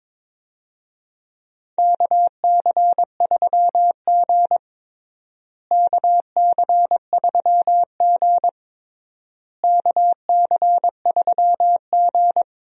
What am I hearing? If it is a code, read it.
Morse code, heard as a keyed tone, about 22 wpm, 700 Hz. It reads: KC3G KC3G KC3G
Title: USA Call Signs - 4 Characters - Repeated 3x - 22wpm